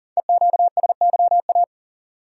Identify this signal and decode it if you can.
Morse code, heard as a keyed tone, about 39 wpm, 690 Hz. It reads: EQSYA
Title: Code Groups - 5 Letters - 40wpm